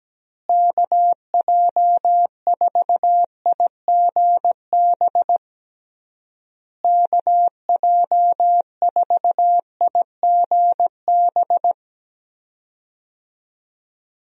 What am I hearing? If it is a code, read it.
Morse code, heard as a keyed tone, about 17 wpm, 700 Hz. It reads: KJ4IGB KJ4IGB
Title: USA Call Signs - 6 Characters - Repeated 3x - 17wpm